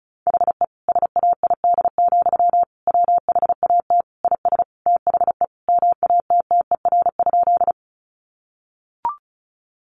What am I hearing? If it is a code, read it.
Morse code, heard as a keyed tone, about 34 wpm, 700 Hz. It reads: HE SAID, WHAT IS THE MATTER?
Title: Sentences from Top 700 Words 35wpm